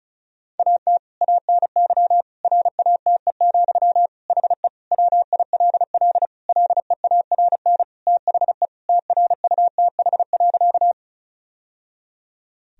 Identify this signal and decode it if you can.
Morse code, heard as a keyed tone, about 34 wpm, 700 Hz. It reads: AT ANY RATE, HE WILL LEARN THE TRUTH.
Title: Sentences from Top 1000 Words - Review Part 4 - 35wpm